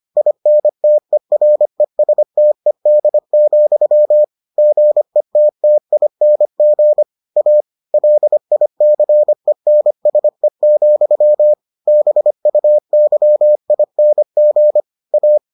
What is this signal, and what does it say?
Morse code, heard as a keyed tone, about 25 wpm, 600 Hz. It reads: INTERESTED, GETTING A LICENSE, BUYING A